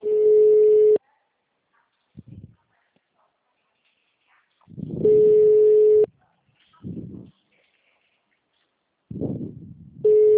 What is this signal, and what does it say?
Ringback tone (Europe, 425 Hz).